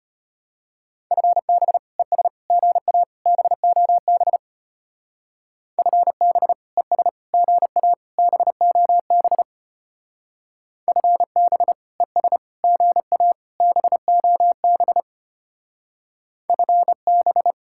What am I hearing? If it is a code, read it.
Morse code, heard as a keyed tone, about 30 wpm, 700 Hz. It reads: FB ES GA BOB FB ES GA BOB FB ES GA BOB FB
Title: Speed-Racing Common QSO Phrases - 25wpm